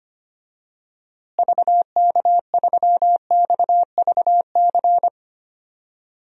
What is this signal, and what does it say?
Morse code, heard as a keyed tone, about 25 wpm, 700 Hz. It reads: VK3XVC
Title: DXCC-100 Call Signs - 6 Characters - Repeated 3x - 25wpm